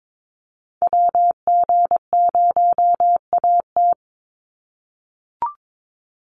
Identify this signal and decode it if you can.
Morse code, heard as a keyed tone, about 22 wpm, 700 Hz. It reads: WG0AT
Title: NA SOTA Activator Callsigns 22wpm